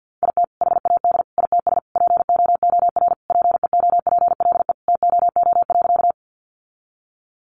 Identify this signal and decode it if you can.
Morse code, heard as a keyed tone, about 50 wpm, 700 Hz. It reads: IT HAD ITS POOR PEOPLE TOO.
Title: Sentences from Top 1000 Words 50wpm